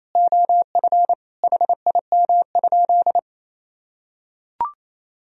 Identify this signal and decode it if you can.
Morse code, heard as a keyed tone, about 28 wpm, 700 Hz. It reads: OF HIM?
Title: Sentences from Top 100 Words and Abbreviations 28wpm